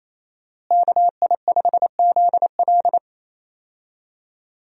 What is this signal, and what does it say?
Morse code, heard as a keyed tone, about 28 wpm, 700 Hz. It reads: KI5ZL
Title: USA Call Signs - 5 Characters - Repeated 2x - 28wpm